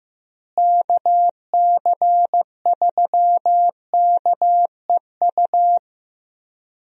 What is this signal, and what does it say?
Morse code, heard as a keyed tone, about 15 wpm, 700 Hz. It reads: KC3KEU
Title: USA Call Signs - 6 Characters - Repeated 3x - 15wpm